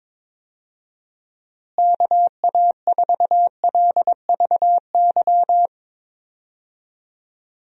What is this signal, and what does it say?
Morse code, heard as a keyed tone, about 21 wpm, 700 Hz. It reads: KA4LVY
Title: USA Call Signs - 6 Characters - Repeated 2x - 22wpm